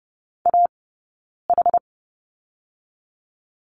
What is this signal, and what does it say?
Morse code, heard as a keyed tone, about 30 wpm, 700 Hz. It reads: A H
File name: Lesson 17d - Letter u - Two Characters 30wpm